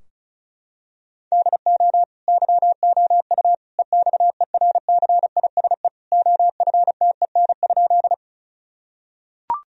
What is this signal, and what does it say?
Morse code, heard as a keyed tone, about 36 wpm, 700 Hz. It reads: DO YOU EXERCISE OFTEN?
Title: Sentences from Top 1000 Words - Review Part 4 - 35wpm